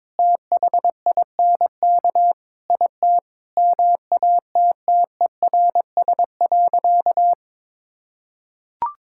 Morse code, heard as a keyed tone, about 22 wpm, 700 Hz. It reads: THINK IT MATTERS.
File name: practice / 22wpm / Sentences with 4-Word Phrases - 22wpm